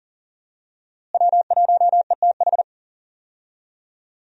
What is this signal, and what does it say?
Morse code, heard as a keyed tone, about 40 wpm, 700 Hz. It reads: W1ETH
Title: USA Call Signs - 5 Characters - Repeated 2x - 40wpm